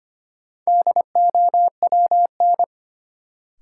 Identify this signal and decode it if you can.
Morse code, heard as a keyed tone, about 25 wpm, 700 Hz. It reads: DOWN
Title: Sentences from Top 300 Words - New Word Drill - 25wpm